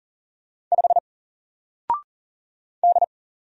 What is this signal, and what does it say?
Morse code, heard as a keyed tone, about 39 wpm, 700 Hz. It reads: 5 D
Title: Single Letter-Number 40wpm